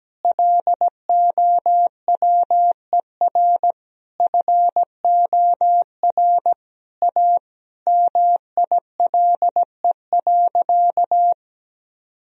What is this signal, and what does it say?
Morse code, heard as a keyed tone, about 17 wpm, 700 Hz. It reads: LOWER FOR A MILE.
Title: Sentences from Top 1000 Words - Review Part 4 - 17wpm